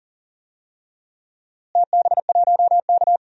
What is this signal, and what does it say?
Morse code, heard as a keyed tone, about 40 wpm, 700 Hz. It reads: TB1K
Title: DXCC-100 Call Signs - 4 Characters - Repeated 2x - 40wpm